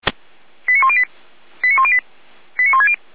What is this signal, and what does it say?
Five-tone selective call (ZVEI), heard as quick runs of five tones, tones of about 70 ms. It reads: 89108 89108 89160
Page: /455/SDR/ZVEI/